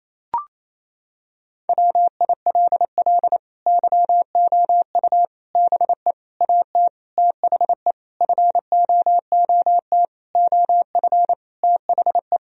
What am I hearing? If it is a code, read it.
Morse code, heard as a keyed tone, about 28 wpm, 700 Hz. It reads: WILL YOU BE AT THE FOOT OF THE
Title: Sentences from Top 1000 Words - Review Part 3 - 28wpm